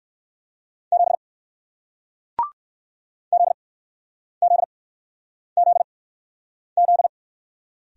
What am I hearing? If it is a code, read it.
Morse code, heard as a keyed tone, about 62 wpm, spaced as for 9 wpm, 700 Hz. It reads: 7 ZZZZ